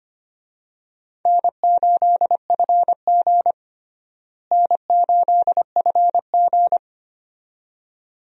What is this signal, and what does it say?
Morse code, heard as a keyed tone, about 25 wpm, 700 Hz. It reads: N8FG N8FG
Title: USA Call Signs - 4 Characters - Repeated 2x - 25wpm